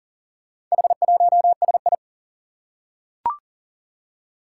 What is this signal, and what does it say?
Morse code, heard as a keyed tone, about 41 wpm, 700 Hz. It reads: H1SI
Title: Four Challenging Letter-Number Combinations 40wpm